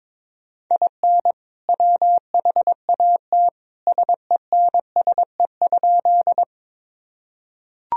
Morse code, heard as a keyed tone, about 21 wpm, 700 Hz. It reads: IN WHAT SENSE?
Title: Sentences from Top 1000 Words - Review Part 4 - 22wpm